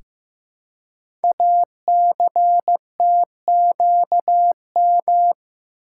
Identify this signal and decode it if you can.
Morse code, heard as a keyed tone, about 15 wpm, 700 Hz. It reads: ACTQM
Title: Code Groups - 5 Letters - 15wpm